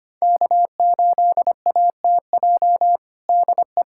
Morse code, heard as a keyed tone, about 25 wpm, 700 Hz. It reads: K8ATJ DE
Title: USA Callsign DE Callsign - 5 Characters - 25wpm